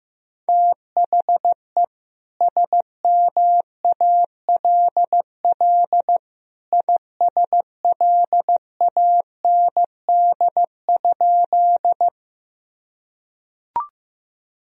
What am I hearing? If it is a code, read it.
Morse code, heard as a keyed tone, about 15 wpm, 700 Hz. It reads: THE SMALL ISLAND?
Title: Sentences from Top 1000 Words - Review Part 3 - 15wpm